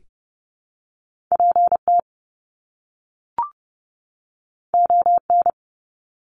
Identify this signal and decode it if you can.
Morse code, heard as a keyed tone, about 30 wpm, 700 Hz. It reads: PT ON
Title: Top Two Letter Combinations in Words 30wpm